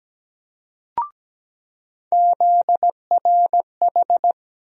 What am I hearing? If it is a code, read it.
Morse code, heard as a keyed tone, about 17 wpm, 700 Hz. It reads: ZRH
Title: USA Call Sign Postfixes - 3 Letters 17wpm